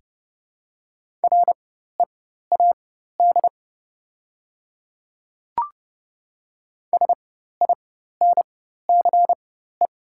Morse code, heard as a keyed tone, about 30 wpm, spaced as for 14 wpm, 700 Hz. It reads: READ SINCE
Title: 4x Character Spacing - Top 300 Words - 30wpm